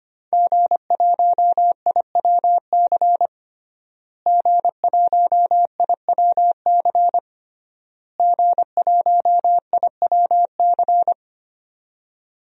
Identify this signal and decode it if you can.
Morse code, heard as a keyed tone, about 25 wpm, 700 Hz. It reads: G1IWC G1IWC G1IWC
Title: DXCC-100 Call Signs - 5 Characters - Repeated 3x - 25wpm